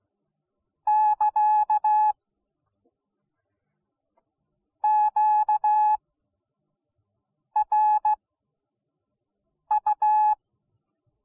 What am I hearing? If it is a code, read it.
Morse code, heard as a keyed tone, about 16 wpm, 850 Hz. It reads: <KA> Q R U